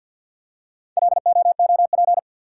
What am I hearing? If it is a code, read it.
Morse code, heard as a keyed tone, about 51 wpm, 700 Hz. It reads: LOOP
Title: Top 180 QSO Elements 50wpm